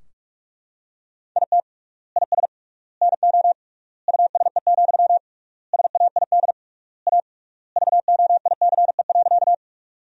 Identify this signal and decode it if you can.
Morse code, heard as a keyed tone, about 41 wpm, 700 Hz. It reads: IT IS NO USE, SAID A VOICE.
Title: 3x Word-Spacing - Sentences from Top 500 Words - 45wpm